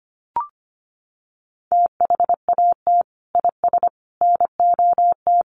Morse code, heard as a keyed tone, about 25 wpm, 700 Hz. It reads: THAT IS NOT
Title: Sentences from Top 700 Words 25wpm